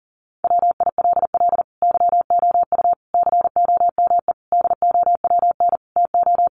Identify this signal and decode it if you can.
Morse code, heard as a keyed tone, about 40 wpm, 700 Hz. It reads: WILL YOU COME DOWN TO
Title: Sentences from Top 1500 Words 40wpm